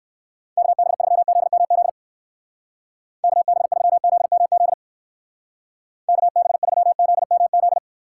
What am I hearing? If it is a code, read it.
Morse code, heard as a keyed tone, about 54 wpm, 700 Hz. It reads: KB3ZMZ KB3ZMZ KB3ZMZ
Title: Speed Racing - USA 2x3 Call Signs 45wpm